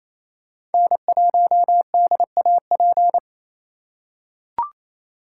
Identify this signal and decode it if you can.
Morse code, heard as a keyed tone, about 28 wpm, 700 Hz. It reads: N1DAP
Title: USA Call Signs - 5 Characters - 6x Character Spacing - 28wpm